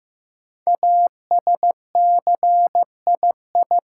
Morse code, heard as a keyed tone, about 15 wpm, 700 Hz. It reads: ASCII
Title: Sets of 5 Words - New Word Drill - 15wpm